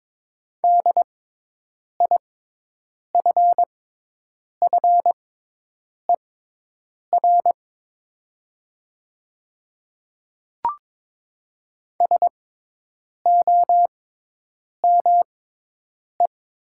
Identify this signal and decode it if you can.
Morse code, heard as a keyed tone, about 22 wpm, spaced as for 8 wpm, 700 Hz. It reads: DIFFER SOME
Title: 6x Character Spacing - Top 200 Words - 22wpm